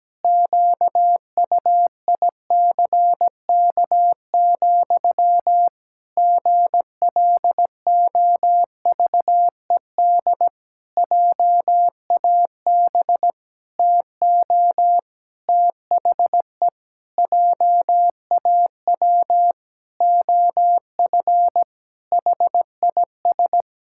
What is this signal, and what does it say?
Morse code, heard as a keyed tone, about 17 wpm, 700 Hz. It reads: QUICK, GLOVED JAB TO THE JAW OF HIS
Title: Pangram Challenge v2 - 17wpm